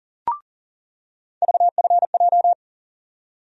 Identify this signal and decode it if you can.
Morse code, heard as a keyed tone, about 40 wpm, 700 Hz. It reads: VFJ
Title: USA Call Sign Postfixes - 3 Letters 40wpm